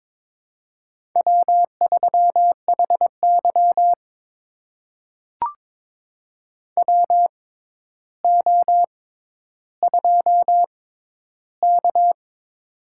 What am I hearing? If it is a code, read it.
Morse code, heard as a keyed tone, about 22 wpm, spaced as for 11 wpm, 700 Hz. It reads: W3HY WO2K
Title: USA Call Signs - 4 Characters - 6x Character Spacing - 22wpm